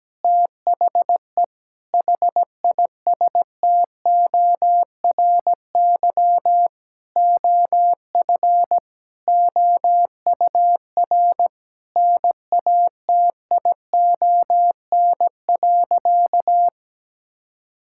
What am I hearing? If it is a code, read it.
Morse code, heard as a keyed tone, about 17 wpm, 700 Hz. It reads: THE HISTORY OF OUR NATION.
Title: Sentences from Top 1000 Words - Review Part 6 - 17wpm